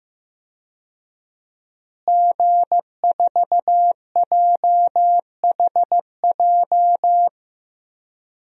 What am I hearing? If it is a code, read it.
Morse code, heard as a keyed tone, about 15 wpm, 700 Hz. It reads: G4JHJ
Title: DXCC-100 Call Signs - 5 Characters - Repeated 3x - 15wpm